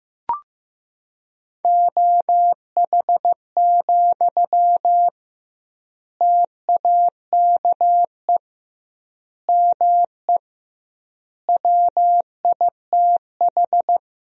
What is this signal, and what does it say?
Morse code, heard as a keyed tone, about 16 wpm, 700 Hz. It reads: OH, TAKE ME WITH
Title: 2x Word-Spacing - Sentences from Top 500 Words - 15wpm